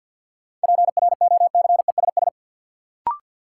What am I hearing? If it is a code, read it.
Morse code, heard as a keyed tone, about 51 wpm, 700 Hz. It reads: PROCESS
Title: Sentences from Top 1000 Words - Review Part 3 - New Word Drill - 50wpm